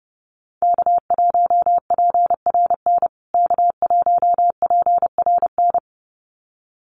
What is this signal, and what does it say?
Morse code, heard as a keyed tone, about 30 wpm, 700 Hz. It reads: K1PRN K1PRN
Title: USA Call Signs Repeated 2x 30wpm